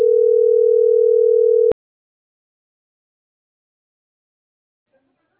Ringback tone (North America), 440 + 480 Hz.